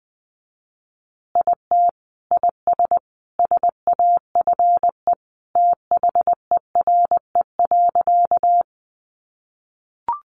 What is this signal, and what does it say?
Morse code, heard as a keyed tone, about 20 wpm, 700 Hz. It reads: IT IS SAFE THERE.
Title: Easy Phrases 20wpm